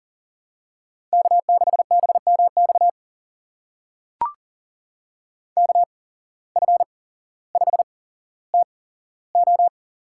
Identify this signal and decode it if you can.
Morse code, heard as a keyed tone, about 39 wpm, spaced as for 17 wpm, 700 Hz. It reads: K6BMX KF5TO
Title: USA Call Signs - 5 Characters - 8x Character Spacing - 40wpm